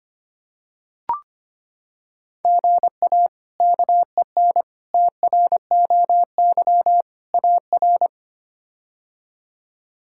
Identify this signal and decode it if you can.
Morse code, heard as a keyed tone, about 25 wpm, 700 Hz. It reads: GA KEN TROY AR
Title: K1USN SST QSO Practice - 25wpm